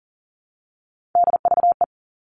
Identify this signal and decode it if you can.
Morse code, heard as a keyed tone, about 40 wpm, 700 Hz. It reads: DVE
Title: All Cut Numbers - 3-Digit 40wpm